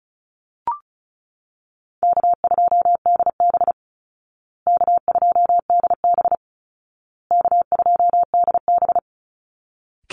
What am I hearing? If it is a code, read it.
Morse code, heard as a keyed tone, about 34 wpm, 700 Hz. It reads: K2DB K2DB K2DB
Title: Easy Call Signs Repeated 3x - 35wpm